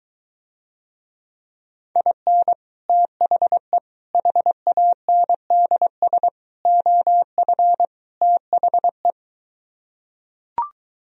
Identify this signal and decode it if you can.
Morse code, heard as a keyed tone, about 23 wpm, 700 Hz. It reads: IN THE HANDS OF THE